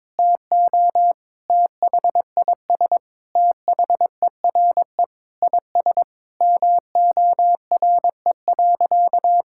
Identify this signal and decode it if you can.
Morse code, heard as a keyed tone, about 22 wpm, 700 Hz. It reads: TO THIS THERE IS MORE.
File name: practice / 22wpm / Sentences with 3-Word Phrases - 22wpm